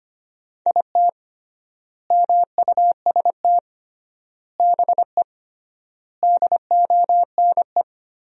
Morse code, heard as a keyed tone, about 25 wpm, 700 Hz. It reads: IT MUST BE DONE
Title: 3x Word-Spacing - Sentences from Top 500 Words - 25wpm